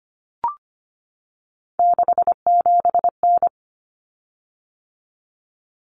Morse code, heard as a keyed tone, about 25 wpm, 700 Hz. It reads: 67N
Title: Common Cut Numbers - 3-Digit 25wpm